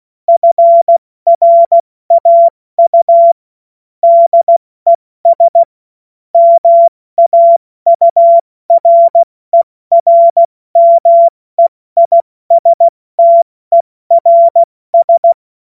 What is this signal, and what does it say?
Morse code, heard as a keyed tone, about 16 wpm, 680 Hz. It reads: FRAU DES MAURERMEISTERS
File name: Morsekurs / Witze / Witz 09. Dezember 2025 70 F80